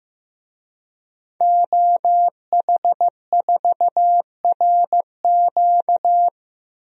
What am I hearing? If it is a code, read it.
Morse code, heard as a keyed tone, about 15 wpm, 700 Hz. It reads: OH4RQ
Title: DXCC-100 Call Signs - 5 Characters - 15wpm